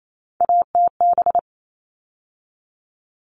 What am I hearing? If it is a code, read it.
Morse code, heard as a keyed tone, about 28 wpm, 700 Hz. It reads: ATB
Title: All Cut Numbers - 3-Digit 28wpm